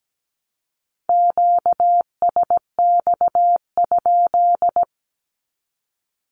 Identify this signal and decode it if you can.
Morse code, heard as a keyed tone, about 17 wpm, 700 Hz. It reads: QSX?
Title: Q-Codes 17wpm